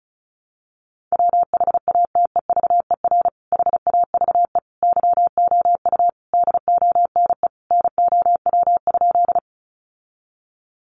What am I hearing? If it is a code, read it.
Morse code, heard as a keyed tone, about 34 wpm, 700 Hz. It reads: WHATEVER HAVE YOU DONE NOW?
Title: Sentences from Top 1100 Words 35wpm